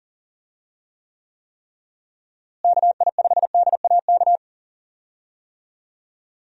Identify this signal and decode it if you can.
Morse code, heard as a keyed tone, about 40 wpm, 700 Hz. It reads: KI5DAK
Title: USA Call Signs - 6 Characters - Repeated 3x - 40wpm